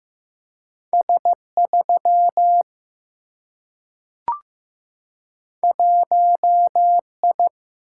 Morse code, heard as a keyed tone, about 15 wpm, 700 Hz. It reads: S 3 1 I
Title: Single-Letter Single-Number 15wpm